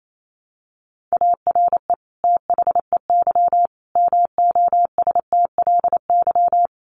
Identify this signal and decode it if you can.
Morse code, heard as a keyed tone, about 28 wpm, 700 Hz. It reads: ARE THEY MOSTLY
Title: Sentences from Top 1300 Words 28wpm